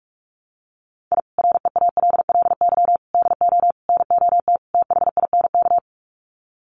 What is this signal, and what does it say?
Morse code, heard as a keyed tone, about 44 wpm, 700 Hz. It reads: I REALLY DO NOT THINK
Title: Sets of 5 Words Encore2 45wpm